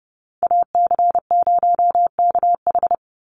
Morse code, heard as a keyed tone, about 30 wpm, 700 Hz. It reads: AC0KH
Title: USA Call Signs 30wpm